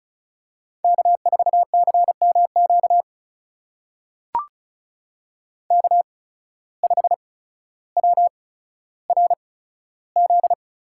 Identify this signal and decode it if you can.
Morse code, heard as a keyed tone, about 36 wpm, spaced as for 15 wpm, 700 Hz. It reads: K4CMQ K5WRZ
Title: USA Call Signs - 5 Characters - 8x Character Spacing - 35wpm